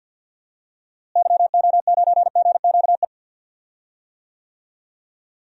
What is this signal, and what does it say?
Morse code, heard as a keyed tone, about 48 wpm, 700 Hz. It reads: YO9GQE
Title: DXCC-100 Call Signs - 6 Characters - Repeated 2x - 50wpm